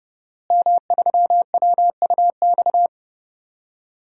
Morse code, heard as a keyed tone, about 30 wpm, 700 Hz. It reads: M3WUX
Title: DXCC-100 Call Signs - 5 Characters - Repeated 3x - 30wpm